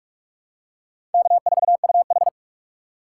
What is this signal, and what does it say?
Morse code, heard as a keyed tone, about 46 wpm, 700 Hz. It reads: K4UH
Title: USA Call Signs - 4 Characters - 2x Character Spacing - 45wpm